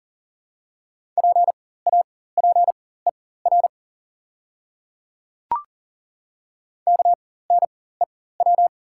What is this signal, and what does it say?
Morse code, heard as a keyed tone, about 40 wpm, spaced as for 19 wpm, 690 Hz. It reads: PAPER KNEW
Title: 4x Character Spacing - Top 500 Words - 40wpm